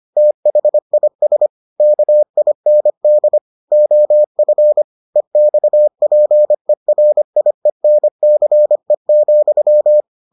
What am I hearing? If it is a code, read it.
Morse code, heard as a keyed tone, about 25 wpm, 600 Hz. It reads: THIS KIND OF EXPERIENCE,